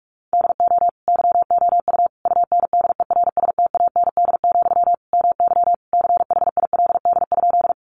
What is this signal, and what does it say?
Morse code, heard as a keyed tone, about 44 wpm, 700 Hz. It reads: DO YOU UNDERSTAND, MY CHILD?
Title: Sentences from Top 800 Words 45wpm